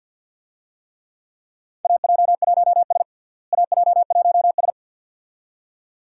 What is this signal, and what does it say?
Morse code, heard as a keyed tone, about 52 wpm, 700 Hz. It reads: AJ1S AJ1S
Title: USA Call Signs - 4 Characters - Repeated 2x - 50wpm